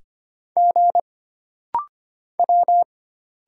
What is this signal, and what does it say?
Morse code, heard as a keyed tone, about 25 wpm, 700 Hz. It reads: G W
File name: Single Character (Letter:Number:Punctuation:Prosign) - Rapid-Fire - 25wpm